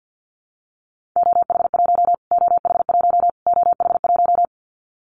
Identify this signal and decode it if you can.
Morse code, heard as a keyed tone, about 50 wpm, 700 Hz. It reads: OH1 OH1 OH1
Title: Callsign Prefixes for Top 25 DXCC - Repeated 3x - 50wpm